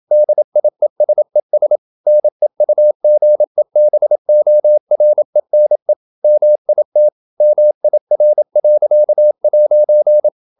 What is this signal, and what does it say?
Morse code, heard as a keyed tone, about 27 wpm, 600 Hz. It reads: DIESES NEUGEBORENE MIT MIR.'